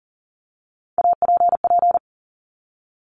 Morse code, heard as a keyed tone, about 40 wpm, 690 Hz. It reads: APP
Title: Top Three Letter Combinations in Words 40wpm